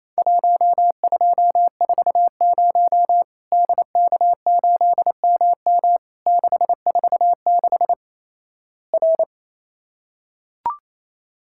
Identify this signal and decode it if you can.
Morse code, heard as a keyed tone, about 28 wpm, 700 Hz. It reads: 1240 DK8MM 646 R
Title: Worked All Europe DX Contest - QTC Exchange - 28wpm